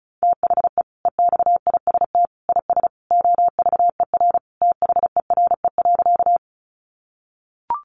Morse code, heard as a keyed tone, about 34 wpm, 700 Hz. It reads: THE EXIST IS OVER THERE.
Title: Sentences from Top 1100 Words 35wpm